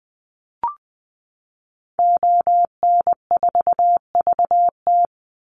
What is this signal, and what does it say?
Morse code, heard as a keyed tone, about 20 wpm, 700 Hz. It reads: ON4VT
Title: Call Signs - 20wpm